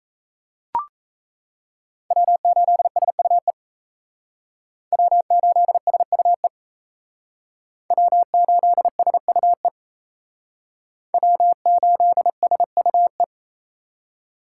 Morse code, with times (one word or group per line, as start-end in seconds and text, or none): 2.1-3.5 W8SUE
4.9-6.5 W8SUE
7.9-9.7 W8SUE
11.1-13.2 W8SUE